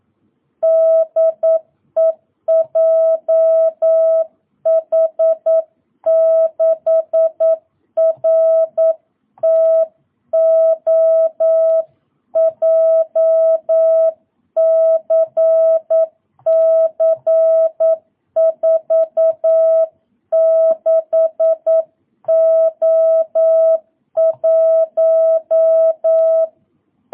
Morse code, heard as a keyed tone, about 9 wpm, 650 Hz. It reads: DE JH6RTO JCC46O1